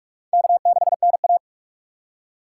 Morse code, heard as a keyed tone, about 46 wpm, 700 Hz. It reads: K6NA
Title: K1USN SST QSO Practice - 45wpm